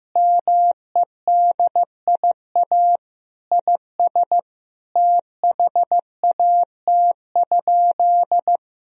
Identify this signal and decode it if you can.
Morse code, heard as a keyed tone, about 15 wpm, 700 Hz. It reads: MEDIA IS THAT?
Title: Sentences from Top 1000 Words - Review Part 4 - 15wpm